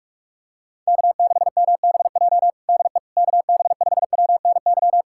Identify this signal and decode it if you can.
Morse code, heard as a keyed tone, about 46 wpm, 700 Hz. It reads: K6MBJ DE KB5WNY
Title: USA Callsign DE Callsign 45wpm